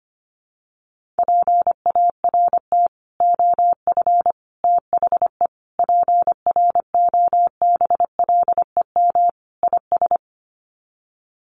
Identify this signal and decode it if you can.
Morse code, heard as a keyed tone, about 25 wpm, 700 Hz. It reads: PART OF THE PROBLEM IS
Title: Sets of 5 Words Encore2 25wpm